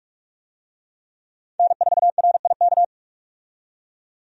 Morse code, heard as a keyed tone, about 46 wpm, 700 Hz. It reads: N4RIK
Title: USA Call Signs - 5 Characters - 45wpm